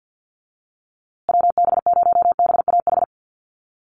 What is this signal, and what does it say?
Morse code, heard as a keyed tone, about 50 wpm, 700 Hz. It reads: WB0BAH
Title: USA Call Signs Repeated 3x 50wpm